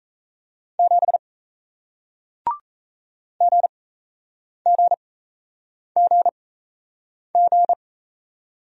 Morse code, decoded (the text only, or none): 7 GGGG